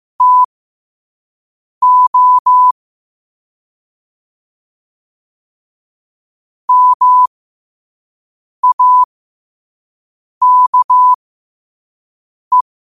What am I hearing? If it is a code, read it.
Morse code, heard as a keyed tone, about 15 wpm, spaced as for 5 wpm, 1000 Hz. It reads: TO MAKE